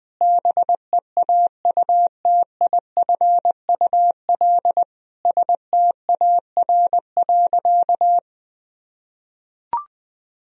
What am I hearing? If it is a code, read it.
Morse code, heard as a keyed tone, about 20 wpm, 700 Hz. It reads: BEAUTIFUL STAR.
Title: Sentences from Top 1000 Words - Review Part 2 - 20wpm